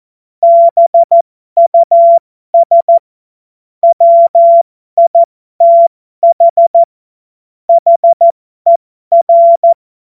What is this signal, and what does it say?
Morse code, heard as a keyed tone, about 14 wpm, spaced as for 12 wpm, 680 Hz. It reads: BUS WITH HER
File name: Morsekurs / Witze